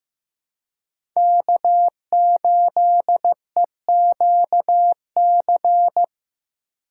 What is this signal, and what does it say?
Morse code, heard as a keyed tone, about 15 wpm, 700 Hz. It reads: K8EQC
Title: USA Call Signs - 5 Characters - 4x Character Spacing - 15wpm